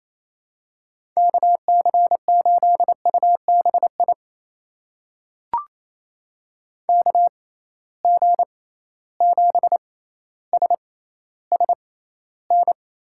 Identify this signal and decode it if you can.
Morse code, heard as a keyed tone, about 28 wpm, spaced as for 14 wpm, 700 Hz. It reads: KC8UBI KG7SSN
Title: USA Call Signs - 6 Characters - 6x Character Spacing - 28wpm